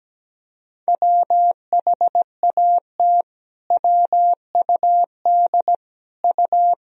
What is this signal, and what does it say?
Morse code, heard as a keyed tone, about 17 wpm, 700 Hz. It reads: WHAT WUD U
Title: Sentences from Top 200 Words and Abbreviations 17wpm